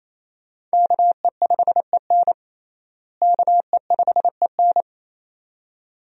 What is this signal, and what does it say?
Morse code, heard as a keyed tone, about 28 wpm, 700 Hz. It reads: KE5EN KE5EN
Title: USA Call Signs - 5 Characters - Repeated 2x - 28wpm